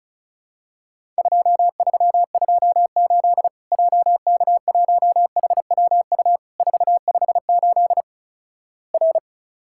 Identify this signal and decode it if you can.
Morse code, heard as a keyed tone, about 36 wpm, 700 Hz. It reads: 2328 JK1HWU 458 R
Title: Worked All Europe DX Contest - QTC Exchange - 35wpm